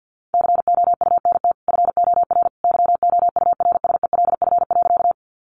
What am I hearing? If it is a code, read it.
Morse code, heard as a keyed tone, about 50 wpm, 700 Hz. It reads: COUNT FOR YOURSELF.
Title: Sentences from Top 1500 Words 50wpm